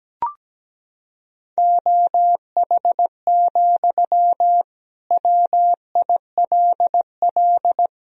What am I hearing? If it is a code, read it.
Morse code, heard as a keyed tone, about 17 wpm, 700 Hz. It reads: OH, WILL
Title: Sentences from Top 1000 Words - Review Part 4 - 17wpm